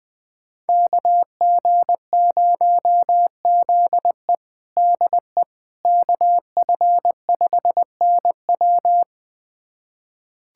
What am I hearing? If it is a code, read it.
Morse code, heard as a keyed tone, about 21 wpm, 700 Hz. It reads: KG0ZE DE KF5NW
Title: USA Callsign DE Callsign - 5 Characters - 20wpm